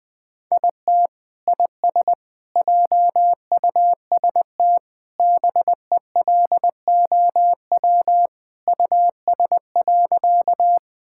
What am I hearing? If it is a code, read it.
Morse code, heard as a keyed tone, about 20 wpm, 700 Hz. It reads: IT IS JUST BELOW US.
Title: Sentences from Top 1000 Words - Review Part 5 - 20wpm